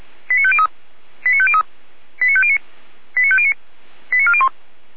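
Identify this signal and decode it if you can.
Five-tone selective call (ZVEI), heard as quick runs of five tones, tones of about 70 ms. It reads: 89503 89503 89509 89509 89401
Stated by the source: ZVEI five-tone call 89262 received by SDR, recorded by feuerwehr-orlamuende.org.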